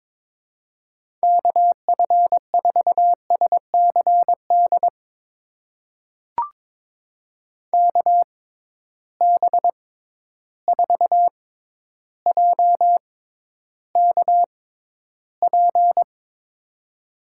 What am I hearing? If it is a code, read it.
Morse code, heard as a keyed tone, about 22 wpm, spaced as for 11 wpm, 700 Hz. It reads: KF4SCD KB4JKP
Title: USA Call Signs - 6 Characters - 6x Character Spacing - 22wpm